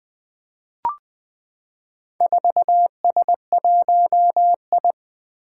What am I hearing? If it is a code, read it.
Morse code, heard as a keyed tone, about 20 wpm, 700 Hz. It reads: 4S1I
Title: Four Challenging Letter-Number Combinations 20wpm